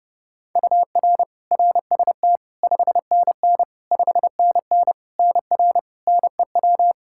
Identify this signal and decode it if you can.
Morse code, heard as a keyed tone, about 29 wpm, 700 Hz. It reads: UR RST 5NN 5NN NR NEW
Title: K1USN SST QSO Practice 30wpm